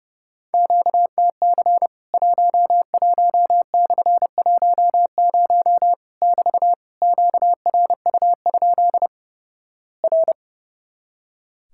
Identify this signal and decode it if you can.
Morse code, heard as a keyed tone, about 30 wpm, 700 Hz. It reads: QTC 11/10 <BT> QRU? R
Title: Worked All Europe DX Contest - QTC Exchange - 30wpm